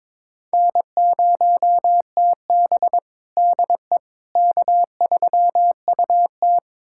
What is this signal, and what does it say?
Morse code, heard as a keyed tone, about 22 wpm, 700 Hz. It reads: N0TB DE K3UT
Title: USA Callsign DE Callsign - 4 Characters - 22wpm